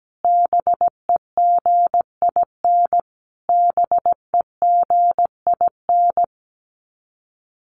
Morse code, heard as a keyed tone, about 17 wpm, 700 Hz. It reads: BEGIN BEGIN
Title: Top 300 Words Repeated 17wpm